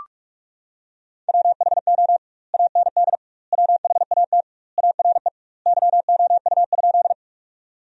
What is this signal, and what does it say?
Morse code, heard as a keyed tone, about 46 wpm, 700 Hz. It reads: WHO AND WHAT ARE YOU?
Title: 2x Word-Spacing - Sentences from Top 100 Words - 45wpm